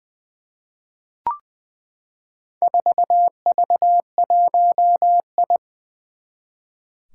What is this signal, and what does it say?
Morse code, heard as a keyed tone, about 20 wpm, 700 Hz. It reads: 4V1I
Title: Four Challenging Letter-Number Combinations 20wpm